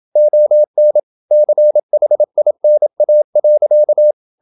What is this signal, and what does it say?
Morse code, heard as a keyed tone, about 27 wpm, 600 Hz. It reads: ON CHINA.
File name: Never 0027wpm